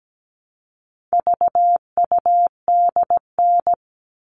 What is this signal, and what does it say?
Morse code, heard as a keyed tone, about 17 wpm, 700 Hz. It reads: VUDN
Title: All Cut Numbers - 4-Digit 17wpm